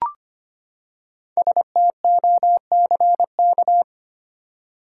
Morse code, heard as a keyed tone, about 25 wpm, 700 Hz. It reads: STOCK
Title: Five Letter Words - 25wpm